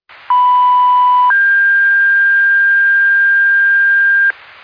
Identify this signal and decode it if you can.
Two-tone page, heard as a one-second tone, then a three-second tone, about 990 then 1600 Hz. It is heard from 0.3 to 4.3 s.